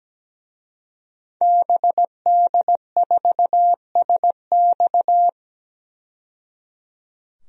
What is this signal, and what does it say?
Morse code, heard as a keyed tone, about 17 wpm, 700 Hz. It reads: BD4SX
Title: DXCC-100 Call Signs - 5 Characters - 17wpm